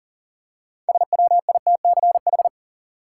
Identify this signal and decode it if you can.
Morse code, heard as a keyed tone, about 39 wpm, 700 Hz. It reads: SWITCH
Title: Six Letter Words - 40wpm